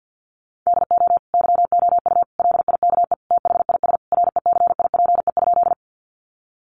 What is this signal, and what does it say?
Morse code, heard as a keyed tone, about 50 wpm, 700 Hz. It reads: DO YOU LIKE THIS RECIPE?
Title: Sentences from Top 1200 Words 50wpm